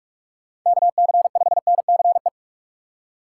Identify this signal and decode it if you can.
Morse code, heard as a keyed tone, about 44 wpm, 700 Hz. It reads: KC5NCE
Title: USA Call Signs - 6 Characters - 6x Character Spacing - 45wpm